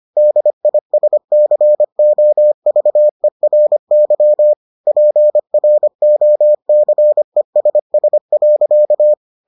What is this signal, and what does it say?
Morse code, heard as a keyed tone, about 25 wpm, 600 Hz. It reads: DISCOVERY PROCESS.